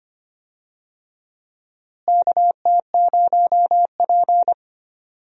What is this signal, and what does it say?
Morse code, heard as a keyed tone, about 25 wpm, 700 Hz. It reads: KT0P
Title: USA Call Signs - 4 Characters - Repeated 3x - 25wpm